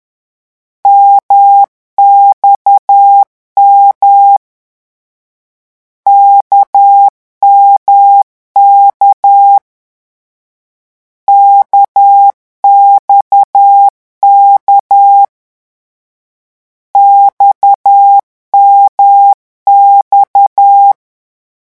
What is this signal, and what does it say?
Morse code, heard as a keyed tone, about 10 wpm, 800 Hz. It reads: MXM KMK KXK XMX
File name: morse mp3 / lecon1-vitesse 10